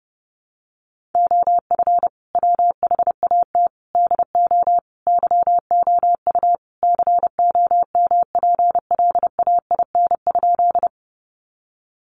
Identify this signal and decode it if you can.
Morse code, heard as a keyed tone, about 30 wpm, 700 Hz. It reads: OF WHAT DO YOU COMPLAIN?
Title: Sentences from Top 1300 Words 30wpm